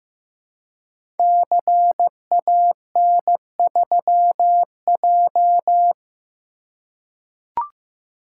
Morse code, heard as a keyed tone, about 15 wpm, 700 Hz. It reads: CAN3J
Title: Code Groups - 5 Letter:Numbers - 15wpm